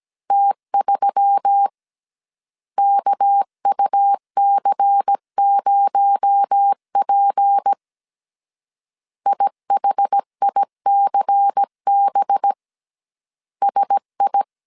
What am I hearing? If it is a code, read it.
Morse code, heard as a keyed tone, about 17 wpm, 800 Hz. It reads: T3 KUC0P IHICB SI
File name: CW / DET / 1 6 1500C S A 16wpm 2635